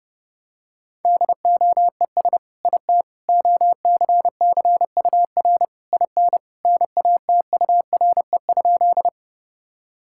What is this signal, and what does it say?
Morse code, heard as a keyed tone, about 32 wpm, 700 Hz. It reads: DOES IT OCCUR IN NATURE?
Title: Sentences from Top 1000 Words - Review Part 6 - 30wpm